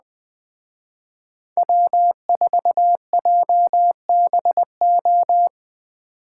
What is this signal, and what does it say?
Morse code, heard as a keyed tone, about 20 wpm, 700 Hz. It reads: W4JBO
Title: Speed Racing - USA 1x3 Call Signs 17wpm